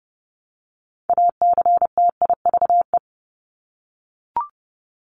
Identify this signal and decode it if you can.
Morse code, heard as a keyed tone, about 30 wpm, 700 Hz. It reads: ACTIVE
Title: Top 751-1000 Words 30wpm